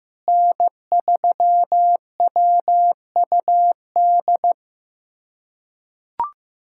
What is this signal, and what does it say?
Morse code, heard as a keyed tone, about 15 wpm, 700 Hz. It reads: N3WUD
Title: USA Callsign DE Callsign - 5 Characters - 15wpm